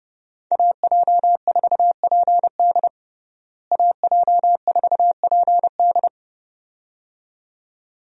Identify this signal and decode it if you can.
Morse code, heard as a keyed tone, about 32 wpm, 700 Hz. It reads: AJ4PD AJ4PD
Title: USA Call Signs - 5 Characters - Repeated 2x - 30wpm